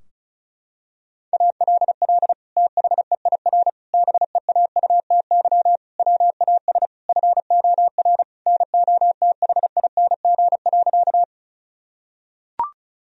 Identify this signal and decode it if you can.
Morse code, heard as a keyed tone, about 36 wpm, 700 Hz. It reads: ALL THEIR BEAUTY WAS FOR NOTHING.
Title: Sentences from Top 1000 Words - Review Part 2 - 35wpm